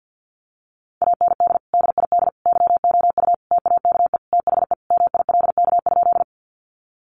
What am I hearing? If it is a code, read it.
Morse code, heard as a keyed tone, about 50 wpm, 700 Hz. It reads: AND DID YOU TAKE THE MILK?
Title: Sentences from Top 1500 Words 50wpm